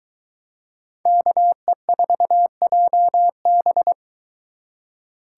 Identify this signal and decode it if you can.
Morse code, heard as a keyed tone, about 21 wpm, 700 Hz. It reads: KE4JB